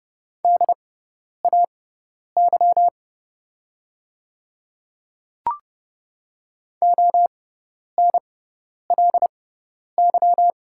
Morse code, heard as a keyed tone, about 30 wpm, spaced as for 10 wpm, 700 Hz. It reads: DAY ONLY
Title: 6x Character Spacing - Top 300 Words - 30wpm